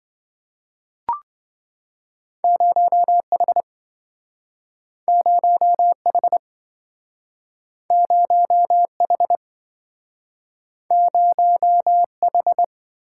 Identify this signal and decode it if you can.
Morse code, heard as a keyed tone, about 27 wpm, 700 Hz. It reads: 0H 0H 0H 0H